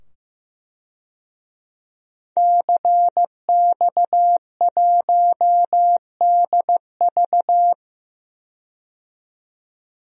Morse code, heard as a keyed tone, about 15 wpm, 700 Hz. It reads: CX1DV
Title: DXCC-100 Call Signs - 5 Characters - 15wpm